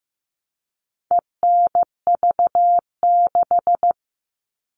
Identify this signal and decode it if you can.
Morse code, heard as a keyed tone, about 15 wpm, 700 Hz. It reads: ENV6
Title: All Cut Numbers - 4-Digit 15wpm